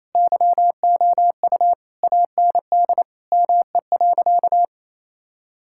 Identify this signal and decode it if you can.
Morse code, heard as a keyed tone, about 28 wpm, 700 Hz. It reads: YOU AND ME.
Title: Sentences from Top 1600 Words 28wpm